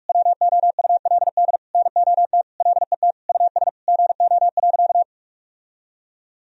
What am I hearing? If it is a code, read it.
Morse code, heard as a keyed tone, about 42 wpm, 700 Hz. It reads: WOULD NOT LET US GO.
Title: Sentences from Top 1000 Words - Review Part 2 - 45wpm